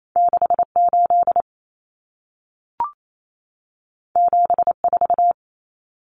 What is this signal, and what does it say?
Morse code, heard as a keyed tone, about 28 wpm, 700 Hz. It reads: 68 74
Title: Numbers - 2-Digit 28wpm